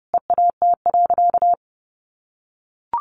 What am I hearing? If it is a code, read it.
Morse code, heard as a keyed tone, about 30 wpm, 700 Hz. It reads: EAT.
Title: Sentences from Top 800 Words 30wpm